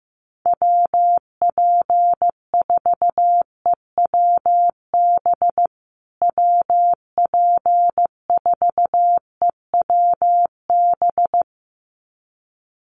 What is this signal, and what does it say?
Morse code, heard as a keyed tone, about 15 wpm, 700 Hz. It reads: WP4EWB WP4EWB
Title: USA Call Signs Repeated 2x 15wpm